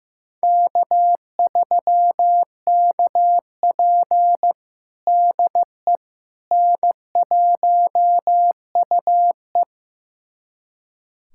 Morse code, heard as a keyed tone, about 15 wpm, 700 Hz. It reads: K3KP DE N1UE
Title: USA Callsign DE Callsign - 4 Characters - 15wpm